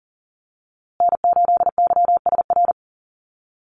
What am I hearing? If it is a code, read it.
Morse code, heard as a keyed tone, about 40 wpm, 700 Hz. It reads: N8YSR
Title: USA Call Signs Repeated 3x 40wpm